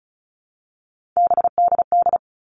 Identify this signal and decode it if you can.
Morse code, heard as a keyed tone, about 34 wpm, 700 Hz. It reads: BDD
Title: All Cut Numbers - 3-Digit 35wpm